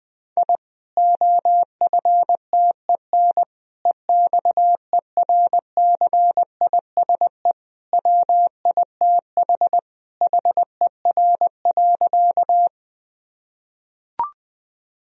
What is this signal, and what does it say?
Morse code, heard as a keyed tone, about 20 wpm, 700 Hz. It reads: I OFTEN EXERCISE WITH HER.
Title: Sentences from Top 1000 Words - Review Part 4 - 20wpm